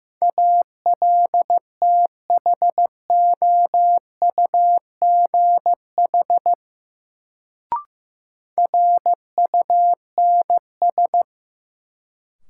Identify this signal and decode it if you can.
Morse code, heard as a keyed tone, about 15 wpm, 700 Hz. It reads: ALTHOUGH RUNS
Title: Sentences from Top 600 Words - New Word Drill - 15wpm